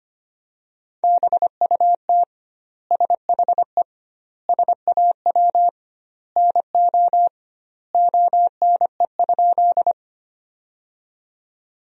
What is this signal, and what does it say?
Morse code, heard as a keyed tone, about 25 wpm, 700 Hz. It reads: BUT SHE SAW NO ONE?
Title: 2x Word-Spacing - Sentences from Top 300 Words - 25wpm